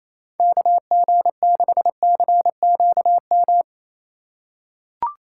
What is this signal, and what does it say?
Morse code, heard as a keyed tone, about 28 wpm, 700 Hz. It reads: KG6CQM
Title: USA Call Signs - 6 Characters - 8x Character Spacing - 28wpm